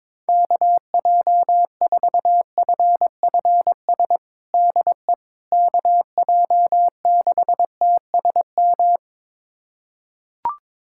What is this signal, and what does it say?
Morse code, heard as a keyed tone, about 22 wpm, 700 Hz. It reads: KJ4FFS DE KJ6TSM
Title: USA Callsign DE Callsign - 6 Characters - 22wpm